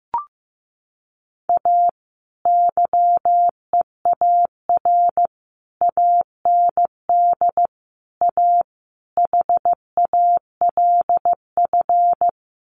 Morse code, heard as a keyed tone, about 15 wpm, 700 Hz. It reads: A YEAR AND A HALF
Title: Sets of 5 Words 15wpm